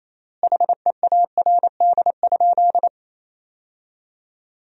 Morse code, heard as a keyed tone, about 28 wpm, 700 Hz. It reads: HEARD?
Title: 4x Word-Spacing - Sentences from Top 400 Words - 28wpm